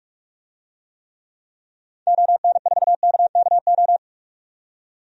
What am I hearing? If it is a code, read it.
Morse code, heard as a keyed tone, about 47 wpm, 700 Hz. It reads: ON4KKO
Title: DXCC-100 Call Signs - 6 Characters - 45wpm